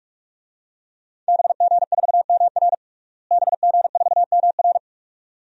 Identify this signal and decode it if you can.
Morse code, heard as a keyed tone, about 43 wpm, 700 Hz. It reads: BG4MR BG4MR
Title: DXCC-100 Call Signs - 5 Characters - Repeated 2x - 45wpm